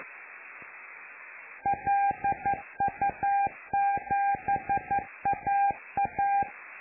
Morse code, heard as a keyed tone, about 11 wpm, 800 Hz. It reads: LU7AA
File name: lu4aao / Globo Amsat